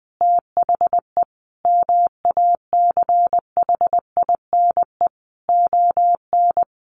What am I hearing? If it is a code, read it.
Morse code, heard as a keyed tone, about 19 wpm, 700 Hz. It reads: THE MACHINE ON
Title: A Sound Of Thunder 20wpm